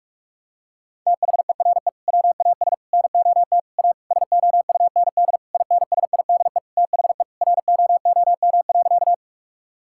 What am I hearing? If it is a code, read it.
Morse code, heard as a keyed tone, about 42 wpm, 700 Hz. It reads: THERE WAS NOT A SOUND INSIDE THE ROOM.
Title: Sentences from Top 1000 Words - Review Part 6 - 45wpm